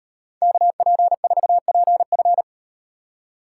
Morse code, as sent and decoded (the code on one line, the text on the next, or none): -.- .--. ....- .--. ..-.
KP4PF